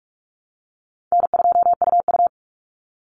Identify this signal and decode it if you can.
Morse code, heard as a keyed tone, about 44 wpm, 700 Hz. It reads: N2UU
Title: Easy Call Signs Repeated 2x - 45wpm